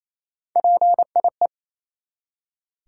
Morse code, heard as a keyed tone, about 28 wpm, 700 Hz. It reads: PIE
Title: Three Letter Words - 28wpm